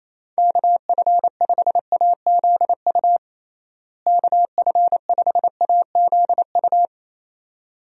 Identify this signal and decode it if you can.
Morse code, heard as a keyed tone, about 28 wpm, 700 Hz. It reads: KF5AZU KF5AZU
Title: USA Call Signs - 6 Characters - Repeated 2x - 28wpm